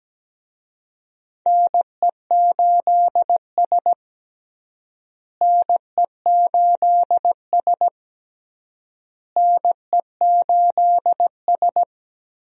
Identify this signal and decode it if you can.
Morse code, heard as a keyed tone, about 17 wpm, 700 Hz. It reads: NE8S NE8S NE8S
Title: USA Call Signs - 4 Characters - Repeated 3x - 17wpm